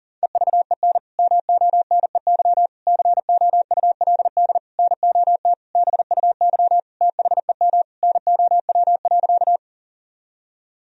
Morse code, heard as a keyed tone, about 40 wpm, 700 Hz. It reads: EVEN MONEY COULD NOT BUY THEM NOW.
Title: Sentences from Top 1000 Words - Review Part 4 - 40wpm